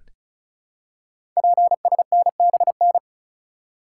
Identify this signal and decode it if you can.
Morse code, heard as a keyed tone, about 36 wpm, 700 Hz. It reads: PSNBN
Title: Code Groups - 5 Letters - 35wpm